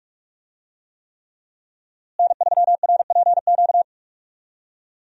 Morse code, heard as a keyed tone, about 46 wpm, 700 Hz. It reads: N3RPQ
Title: USA Call Signs - 5 Characters - Repeated 3x - 45wpm